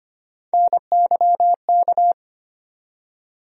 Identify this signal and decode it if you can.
Morse code, heard as a keyed tone, about 25 wpm, 700 Hz. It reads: NYK
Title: USA Call Sign Postfixes - 3 Letters 25wpm